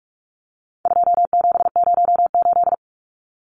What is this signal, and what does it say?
Morse code, heard as a keyed tone, about 44 wpm, 700 Hz. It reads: 2708
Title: Numbers - 4-Digit 45wpm